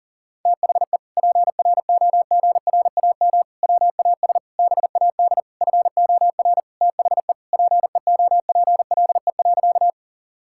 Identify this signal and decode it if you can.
Morse code, heard as a keyed tone, about 40 wpm, 700 Hz. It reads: THE PROGRAM WAS BAD FOR THE PEOPLE.
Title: Sentences from Top 1000 Words - Review Part 5 - 40wpm